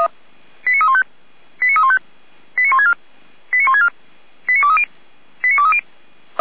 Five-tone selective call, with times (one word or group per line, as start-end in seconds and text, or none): 0.7-1.0 89315
1.6-2.0 89315
2.6-2.9 89164
3.5-3.9 89164
4.5-4.8 89230
5.4-5.8 89230